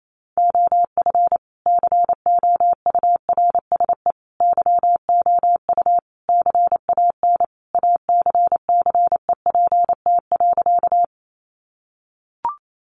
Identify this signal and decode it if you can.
Morse code, heard as a keyed tone, about 28 wpm, 700 Hz. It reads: OF COURSE YOU CAN ACCEPT.
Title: Sentences from Top 1300 Words 28wpm